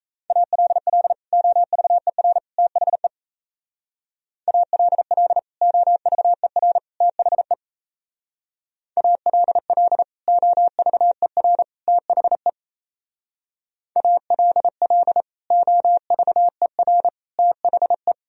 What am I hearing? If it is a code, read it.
Morse code, heard as a keyed tone, about 33 wpm, 700 Hz. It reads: ALL OVER THE ALL OVER THE ALL OVER THE ALL OVER THE
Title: Speed-Racing - Sets of 3 Words 28wpm